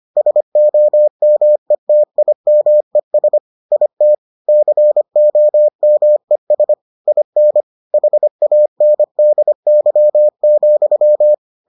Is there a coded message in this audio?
Yes: Morse code from 0.2 to 11.3 s, heard as a keyed tone, about 25 wpm, 600 Hz. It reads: SOMETIMES IT COMES IN HANDY,